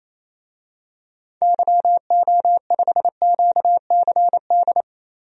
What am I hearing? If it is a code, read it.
Morse code, heard as a keyed tone, about 28 wpm, 700 Hz. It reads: YO5QCD
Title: DXCC-100 Call Signs - 6 Characters - Repeated 2x - 28wpm